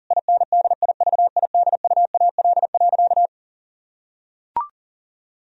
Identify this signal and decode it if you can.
Morse code, heard as a keyed tone, about 40 wpm, 700 Hz. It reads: INDIVIDUAL.
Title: Sentences from Top 1000 Words - Review Part 4 - 40wpm